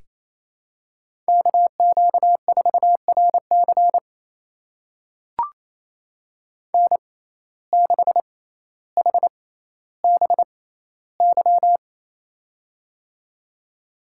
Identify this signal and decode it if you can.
Morse code, heard as a keyed tone, about 28 wpm, spaced as for 14 wpm, 700 Hz. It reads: KQ4RC N6HBY
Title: USA Call Signs - 5 Characters - 6x Character Spacing - 28wpm